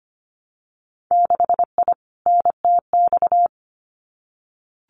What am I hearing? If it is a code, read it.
Morse code, heard as a keyed tone, about 25 wpm, 700 Hz. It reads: 6I NTX
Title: Winter Field Day Exchanges 25wpm